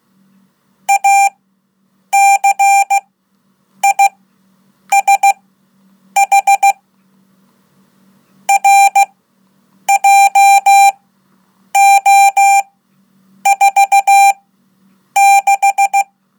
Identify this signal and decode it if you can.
Morse code, heard as a keyed tone, about 16 wpm, spaced as for 8 wpm, 790 Hz. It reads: ACISH RJO46
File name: Code / 5min-1